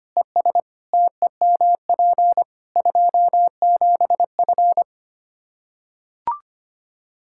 Morse code, heard as a keyed tone, about 25 wpm, 700 Hz. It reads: ES TEMP 27F
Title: Practice QSO - Exchange 2 - Info - 25wpm